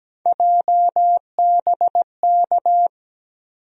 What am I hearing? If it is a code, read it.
Morse code, heard as a keyed tone, about 17 wpm, 700 Hz. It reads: JBK
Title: USA Call Sign Postfixes - 3 Letters 17wpm